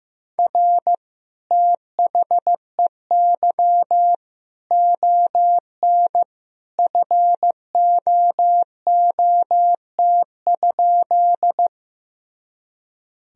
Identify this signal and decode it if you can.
Morse code, heard as a keyed tone, about 15 wpm, 700 Hz. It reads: R THEY ON FOOT?
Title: Sentences from Top 500 Words and Abbreviations 15wpm